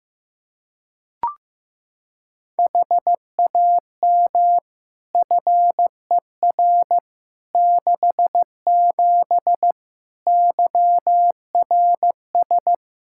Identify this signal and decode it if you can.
Morse code, heard as a keyed tone, about 15 wpm, 700 Hz. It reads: HAM FER 67 YRS
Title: Practice QSO - Exchange 3 - More Info - 15wpm